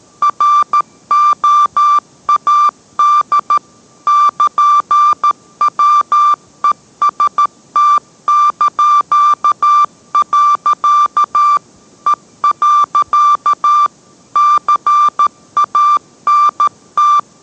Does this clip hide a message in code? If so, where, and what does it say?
Yes: Morse code from 0.2 to 17.2 s, heard as a keyed tone, about 13 wpm, 1180 Hz. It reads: ROAD <KN>WEST). E. CANT